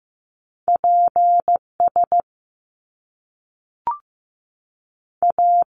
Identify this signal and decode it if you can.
Morse code, heard as a keyed tone, about 15 wpm, 700 Hz. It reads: P S A
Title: Top Two Letter Combinations in Words 15wpm